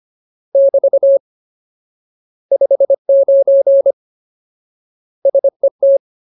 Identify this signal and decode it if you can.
Morse code, heard as a keyed tone, about 25 wpm, 550 Hz. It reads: <BT> 59 SET
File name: Ultimate Word List / 25WPM Ultimate